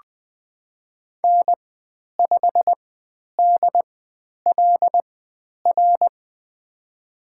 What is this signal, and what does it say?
Morse code, heard as a keyed tone, about 20 wpm, spaced as for 10 wpm, 700 Hz. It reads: N5DLR
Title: 109 Lesson 18g - Letter m - Call Signs 20wpm